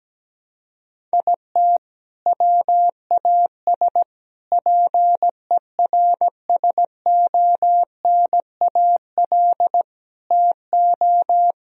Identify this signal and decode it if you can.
Morse code, heard as a keyed tone, about 17 wpm, 700 Hz. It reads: IT WAS PERSONAL TO
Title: Sentences from Top 1000 Words - Review Part 4 - 17wpm